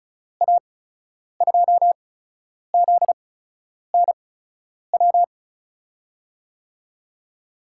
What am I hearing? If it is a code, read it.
Morse code, heard as a keyed tone, about 36 wpm, 700 Hz. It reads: A 2 Z N W
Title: USA Call Signs - 8x Character Spacing - 35wpm